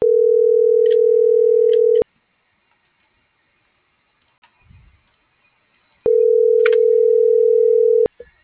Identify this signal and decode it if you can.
Ringback tone (North America), 440 + 480 Hz.